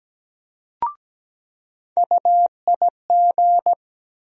Morse code, heard as a keyed tone, about 17 wpm, 700 Hz. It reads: UIG
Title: USA Call Sign Postfixes - 3 Letters 17wpm